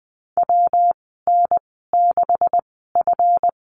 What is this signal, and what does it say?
Morse code, heard as a keyed tone, about 20 wpm, spaced as for 14 wpm, 700 Hz. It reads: WN6F
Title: 161 Lesson 26g - Letter g - Call Signs 20wpm